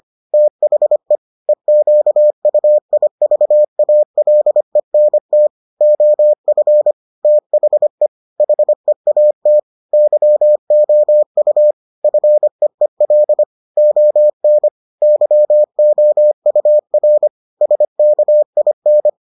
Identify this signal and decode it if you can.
Morse code, heard as a keyed tone, about 25 wpm, 600 Hz. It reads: THE EQUIVALENT OF THE HEAT YOU FEEL ON YOUR SKIN